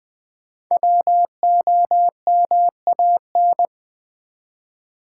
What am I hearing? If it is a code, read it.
Morse code, heard as a keyed tone, about 20 wpm, 700 Hz. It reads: WOMAN
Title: Sentences from Top 1100 Words - New Word Drill Alt v2 - 20wpm